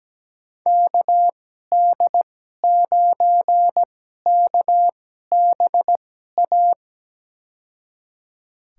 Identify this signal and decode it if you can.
Morse code, heard as a keyed tone, about 17 wpm, spaced as for 12 wpm, 700 Hz. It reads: KD9KBA
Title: USA Call Signs - 6 Characters - 2x Character Spacing - 17wpm